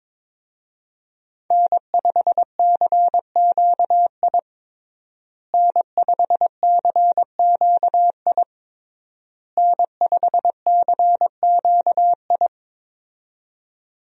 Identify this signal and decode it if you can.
Morse code, heard as a keyed tone, about 22 wpm, 700 Hz. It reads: N5CQI N5CQI N5CQI
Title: USA Call Signs - 5 Characters - Repeated 3x - 22wpm